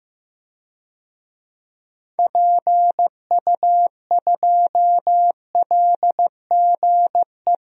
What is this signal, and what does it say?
Morse code, heard as a keyed tone, about 15 wpm, 700 Hz. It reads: PU2LGE
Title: DXCC-100 Call Signs - 6 Characters - Repeated 2x - 15wpm